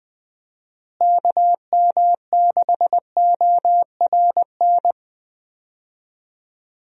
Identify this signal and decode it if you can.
Morse code, heard as a keyed tone, about 20 wpm, 700 Hz. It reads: KM6ORN
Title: USA Call Signs - 6 Characters - Repeated 2x - 20wpm